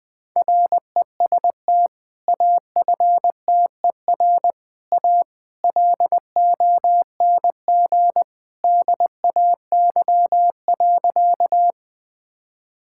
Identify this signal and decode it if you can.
Morse code, heard as a keyed tone, about 20 wpm, 700 Hz. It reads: REST AFTER A LONG DAY.
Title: Sentences with 2-Word Phrases - Spoken Beginning - 20wpm